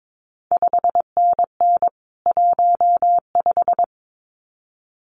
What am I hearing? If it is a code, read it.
Morse code, heard as a keyed tone, about 22 wpm, 700 Hz. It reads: 5NN 15
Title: CQ WW DX Contest Exchange Encore - 22wpm